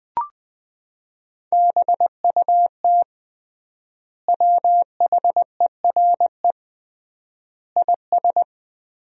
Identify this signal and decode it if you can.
Morse code, heard as a keyed tone, about 20 wpm, 700 Hz. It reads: BUT WHERE IS
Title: 3x Word-Spacing - Sentences from Top 200 Words - 20wpm